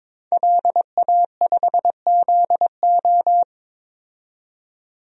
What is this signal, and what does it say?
Morse code, heard as a keyed tone, about 22 wpm, 700 Hz. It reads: LA5ZO
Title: DXCC-100 Call Signs - 5 Characters - 22wpm